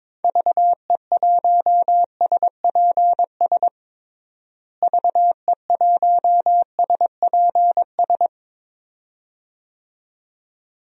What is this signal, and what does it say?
Morse code, heard as a keyed tone, about 22 wpm, 700 Hz. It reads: VE1SPS VE1SPS
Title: DXCC-100 Call Signs - 6 Characters - Repeated 2x - 22wpm